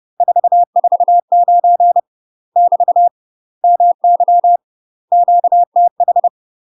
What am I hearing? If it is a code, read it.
Morse code, heard as a keyed tone, about 30 wpm, 700 Hz. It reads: 449 <BT> MY QTH
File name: UK MASTER QSOs TRAINING/30WPM